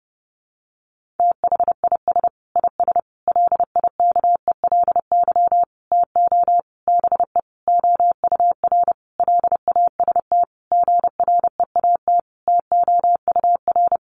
Morse code, heard as a keyed tone, about 30 wpm, 700 Hz. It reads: THIS IS LIKELY TO BE OUR LAST GREAT TOUR
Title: Sentences from Top 1400 Words 30wpm